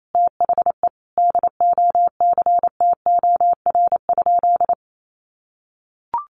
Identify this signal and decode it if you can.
Morse code, heard as a keyed tone, about 28 wpm, 700 Hz. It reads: THE DOCTOR?
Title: Sentences from Top 1500 Words 28wpm